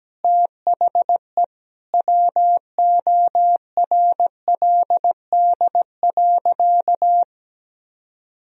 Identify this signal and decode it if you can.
Morse code, heard as a keyed tone, about 17 wpm, 700 Hz. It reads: THE WORLD.
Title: Sentences from Top 200 Words and Abbreviations 17wpm